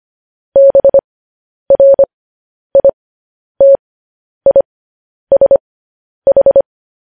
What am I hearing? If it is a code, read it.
Morse code, heard as a keyed tone, about 25 wpm, spaced as for 10 wpm, 550 Hz. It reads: BRITISH
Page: news headlines in morse code at 10 wpm